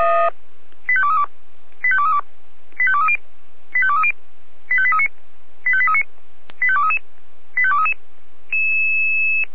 Five-tone selective call (ZVEI), heard as quick runs of five tones, tones of about 70 ms. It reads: 85232 85232 85230 85230 85839 85839 85233 85233